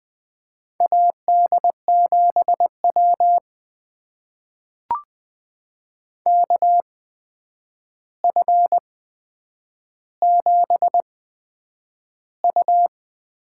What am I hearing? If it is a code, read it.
Morse code, heard as a keyed tone, about 20 wpm, spaced as for 9 wpm, 700 Hz. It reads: AD7W KF7U
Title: USA Call Signs - 8x Character Spacing - 20wpm